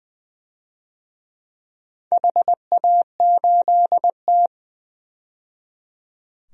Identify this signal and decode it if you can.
Morse code, heard as a keyed tone, about 20 wpm, 700 Hz. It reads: HA8T
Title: DXCC-100 Call Signs - 4 Characters - 20wpm